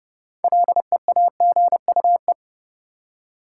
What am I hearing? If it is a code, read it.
Morse code, heard as a keyed tone, about 30 wpm, 700 Hz. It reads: LEAGUE
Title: Six Letter Words - 30wpm